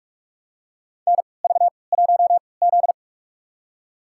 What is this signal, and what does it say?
Morse code, heard as a keyed tone, about 43 wpm, spaced as for 25 wpm, 690 Hz. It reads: NV1Z
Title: USA Call Signs - 4 Characters - 3x Character Spacing - 45wpm